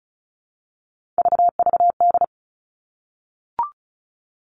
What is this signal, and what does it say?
Morse code, heard as a keyed tone, about 36 wpm, 700 Hz. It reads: VVD
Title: Three Challenging Letter-Number Combinations 35wpm